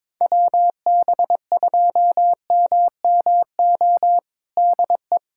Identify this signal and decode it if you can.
Morse code, heard as a keyed tone, about 22 wpm, 700 Hz. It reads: WB2MMO DE
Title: USA Callsign DE Callsign - 6 Characters - 22wpm